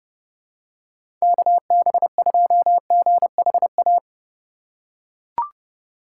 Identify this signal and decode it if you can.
Morse code, heard as a keyed tone, about 32 wpm, 700 Hz. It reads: KB2GHA
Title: USA Call Signs - 6 Characters - 3x Character Spacing - 30wpm